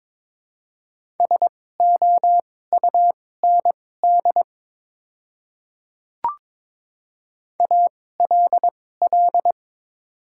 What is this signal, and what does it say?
Morse code, heard as a keyed tone, about 22 wpm, spaced as for 16 wpm, 700 Hz. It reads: SOUND ALL
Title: 2x Character Spacing - Top 100 Words - 22wpm